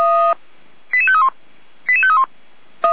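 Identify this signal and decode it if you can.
Five-tone selective call (ZVEI), heard as quick runs of five tones, tones of about 70 ms. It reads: 88531 88531
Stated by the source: ZVEI five-tone selective call received with an SDR.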